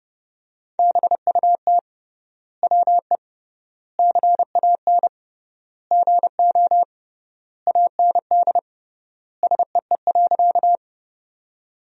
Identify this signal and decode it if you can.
Morse code, heard as a keyed tone, about 30 wpm, 700 Hz. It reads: BUT WE CAN GO AND SEE.
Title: 3x Word-Spacing - Sentences from Top 100 Words - 30wpm